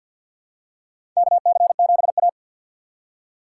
Morse code, heard as a keyed tone, about 50 wpm, 700 Hz. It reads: KC7A
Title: USA Call Signs - 4 Characters - 2x Character Spacing - 50wpm